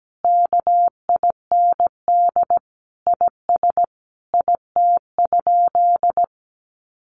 Morse code, heard as a keyed tone, about 17 wpm, 700 Hz. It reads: KIND IS IT?
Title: Sentences from Top 200 Words 17wpm